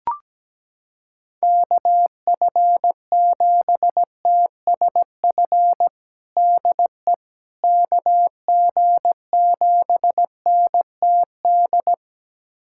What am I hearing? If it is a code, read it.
Morse code, heard as a keyed tone, about 17 wpm, 700 Hz. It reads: KF7TSF DE KG7NTD
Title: USA Callsign DE Callsign - 6 Characters - 17wpm